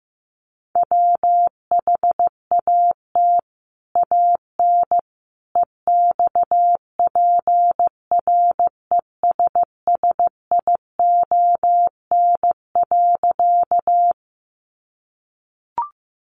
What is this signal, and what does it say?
Morse code, heard as a keyed tone, about 15 wpm, 700 Hz. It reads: WHAT AN EXPRESSION.
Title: Sentences from Top 1200 Words 15wpm